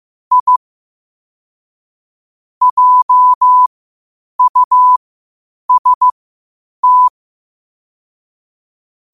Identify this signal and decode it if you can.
Morse code, heard as a keyed tone, about 15 wpm, spaced as for 8 wpm, 1000 Hz. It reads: I JUST
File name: qotdfiles / 8wpm